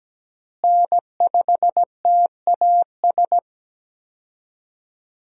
Morse code, heard as a keyed tone, about 17 wpm, 700 Hz. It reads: N5TAS
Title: USA Call Signs - 5 Characters - Repeated 3x - 17wpm